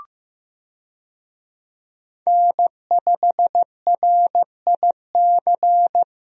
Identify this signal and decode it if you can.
Morse code, heard as a keyed tone, about 15 wpm, 700 Hz. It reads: N5RIC
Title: USA Call Signs - 5 Characters - Repeated 3x - 15wpm